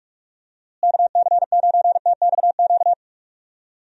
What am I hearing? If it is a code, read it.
Morse code, heard as a keyed tone, about 46 wpm, 700 Hz. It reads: KC9TXQ
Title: USA Call Signs - 6 Characters - Repeated 2x - 45wpm